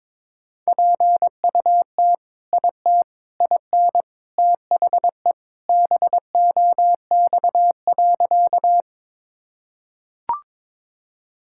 Morse code, heard as a keyed tone, about 21 wpm, 700 Hz. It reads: PUT IT IN THE BOX.
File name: practice / 22wpm / Sentences with 3-Word Phrases - 22wpm